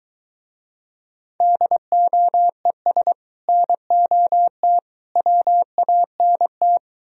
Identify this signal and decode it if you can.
Morse code, heard as a keyed tone, about 22 wpm, 700 Hz. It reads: DOES NOT WANT